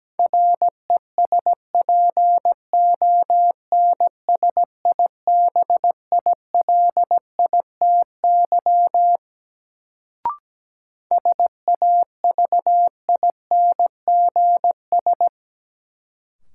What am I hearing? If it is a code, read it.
Morse code, heard as a keyed tone, about 17 wpm, 700 Hz. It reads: RESPONSIBILITY SAVINGS
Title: Sentences from Top 1200 Words - New Word Drill - 17wpm